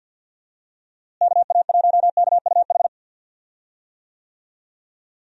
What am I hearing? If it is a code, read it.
Morse code, heard as a keyed tone, about 50 wpm, 700 Hz. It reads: KA1KUH
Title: USA Call Signs - 6 Characters - Repeated 3x - 50wpm